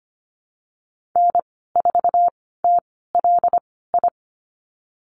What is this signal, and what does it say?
Morse code, heard as a keyed tone, about 25 wpm, spaced as for 16 wpm, 700 Hz. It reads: N4TLI
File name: Lesson 15g - Number 5 - Call Signs 25wpm